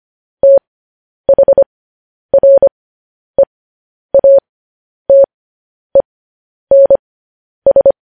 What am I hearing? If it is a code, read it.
Morse code, heard as a keyed tone, about 25 wpm, spaced as for 10 wpm, 550 Hz. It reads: THREATENS
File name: podcast / News headlines-1x-25x10